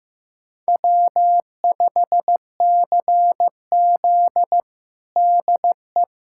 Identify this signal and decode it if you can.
Morse code, heard as a keyed tone, about 15 wpm, 700 Hz. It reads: W5CZ DE
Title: USA Callsign DE Callsign - 4 Characters - 15wpm